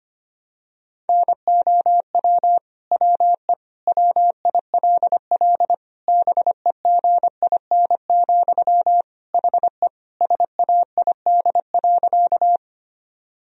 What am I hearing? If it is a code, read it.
Morse code, heard as a keyed tone, about 25 wpm, 700 Hz. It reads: NOW WE WILL BEGIN, HE SAID.
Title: Sentences from Top 1000 Words - Review Part 1 - 25wpm